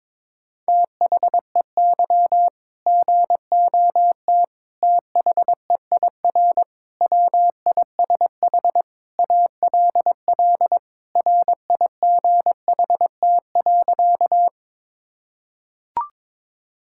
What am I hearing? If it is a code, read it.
Morse code, heard as a keyed tone, about 21 wpm, 700 Hz. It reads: THEY GOT THEIR WISH ALL RIGHT.
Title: Sentences from Top 1600 Words 22wpm